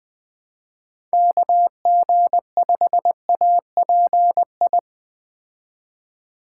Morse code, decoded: KG5API

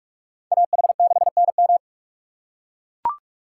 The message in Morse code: AH6NM

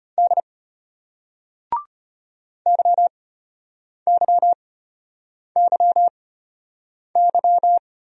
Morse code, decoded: D YYYY

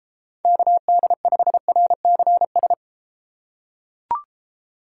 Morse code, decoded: KD5RCS

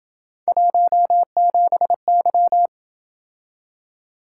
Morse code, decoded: 17Y

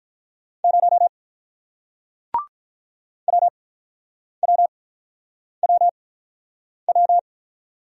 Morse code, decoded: 0 WWWW

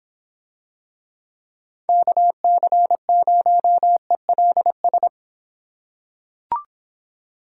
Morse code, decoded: KC0ELS

